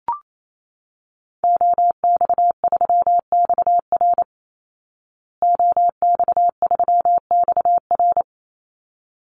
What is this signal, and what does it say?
Morse code, heard as a keyed tone, about 28 wpm, 700 Hz. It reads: OX3XR OX3XR
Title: Call Signs Repeated 3x - 28wpm